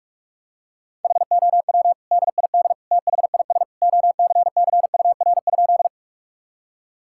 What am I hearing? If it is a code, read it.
Morse code, heard as a keyed tone, about 46 wpm, 700 Hz. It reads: HOW DID THIS OCCUR?
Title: Sentences from Top 1000 Words - Review Part 6 - 45wpm